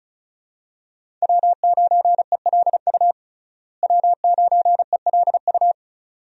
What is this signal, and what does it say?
Morse code, heard as a keyed tone, about 36 wpm, 700 Hz. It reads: W9ELU W9ELU
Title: USA Call Signs - 5 Characters - Repeated 2x - 35wpm